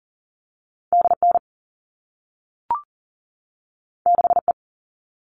Morse code, heard as a keyed tone, about 41 wpm, 700 Hz. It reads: DN 6E
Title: All Cut Numbers - 2-Digit 40wpm